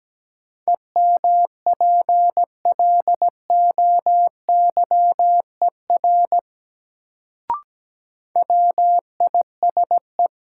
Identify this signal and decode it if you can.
Morse code, heard as a keyed tone, about 17 wpm, 700 Hz. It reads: EMPLOYER WISE
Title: Sentences from Top 1600 Words - New Word Drill Alt - 17wpm